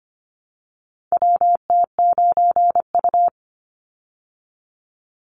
Morse code, heard as a keyed tone, about 25 wpm, 700 Hz. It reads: WT9U
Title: Call Signs Repeated 3x - 25wpm